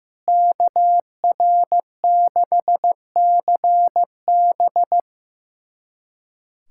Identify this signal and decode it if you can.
Morse code, heard as a keyed tone, about 16 wpm, 700 Hz. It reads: KR6CB
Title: USA Callsign DE Callsign - 5 Characters - 15wpm